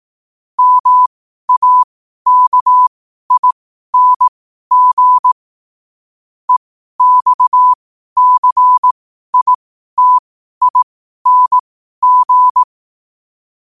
Morse code, heard as a keyed tone, about 18 wpm, spaced as for 12 wpm, 1000 Hz. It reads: MAKING EXCITING
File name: qotdfiles / 12wpm